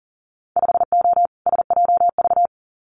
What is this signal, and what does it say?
Morse code, heard as a keyed tone, about 41 wpm, 700 Hz. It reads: 5O SJV
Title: Winter Field Day Exchanges Encore 40wpm